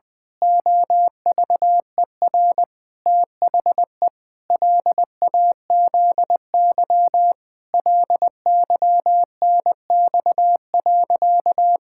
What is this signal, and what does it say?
Morse code, heard as a keyed tone, about 20 wpm, 700 Hz. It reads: OVER THE LAZY LYNX.
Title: Pangram Challenge v2 - 20wpm